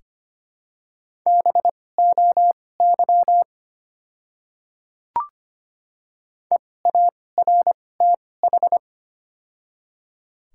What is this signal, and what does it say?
Morse code, heard as a keyed tone, about 25 wpm, spaced as for 18 wpm, 700 Hz. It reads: BOY EARTH
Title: 2x Character Spacing - Top 200 Words - 25wpm